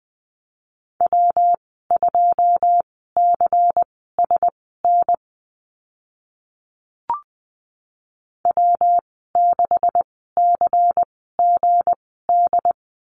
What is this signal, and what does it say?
Morse code, heard as a keyed tone, about 20 wpm, spaced as for 14 wpm, 700 Hz. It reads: W2CSN W6CGD
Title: 161 Lesson 26g - Letter g - Call Signs 20wpm